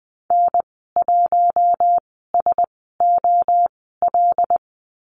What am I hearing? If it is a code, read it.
Morse code, heard as a keyed tone, about 20 wpm, spaced as for 14 wpm, 700 Hz. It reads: N1SOL
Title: 076 Lesson 13g - Letter L - Call Signs 20wpm